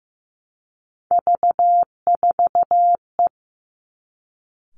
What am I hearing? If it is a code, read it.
Morse code, heard as a keyed tone, about 15 wpm, 700 Hz. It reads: V4E
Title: Three Challenging Letter-Number Combinations 15wpm